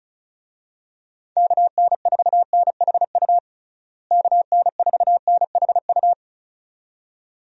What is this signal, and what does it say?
Morse code, heard as a keyed tone, about 36 wpm, 700 Hz. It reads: KN4NHU KN4NHU
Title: USA Call Signs - 6 Characters - Repeated 2x - 35wpm